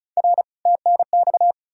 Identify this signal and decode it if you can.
Morse code, heard as a keyed tone, about 46 wpm, 700 Hz. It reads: R TNX